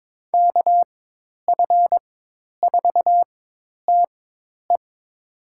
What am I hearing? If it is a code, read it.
Morse code, heard as a keyed tone, about 22 wpm, spaced as for 10 wpm, 700 Hz. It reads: KF4TE
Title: USA Call Signs - 5 Characters - 4x Character Spacing - 22wpm